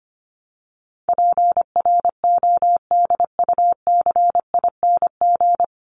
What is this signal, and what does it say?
Morse code, heard as a keyed tone, about 25 wpm, 700 Hz. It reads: PRODUCING
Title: ING Suffix - Top 100 Words with Base Words 25wpm